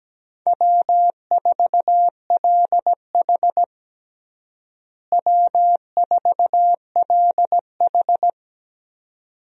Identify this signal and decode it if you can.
Morse code, heard as a keyed tone, about 17 wpm, 700 Hz. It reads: W4LH W4LH
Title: USA Call Signs - 4 Characters - Repeated 2x - 17wpm